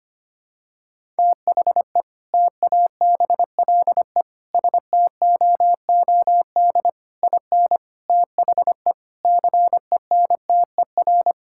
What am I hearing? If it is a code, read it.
Morse code, heard as a keyed tone, about 25 wpm, 700 Hz. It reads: THE TABLE STOOD IN THE CENTER
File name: practice / 25wpm / Sentences with 4-Word Phrases - 25wpm